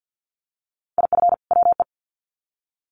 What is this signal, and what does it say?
Morse code, heard as a keyed tone, about 50 wpm, 700 Hz. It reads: IF WE
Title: Sets of 2 Words 50wpm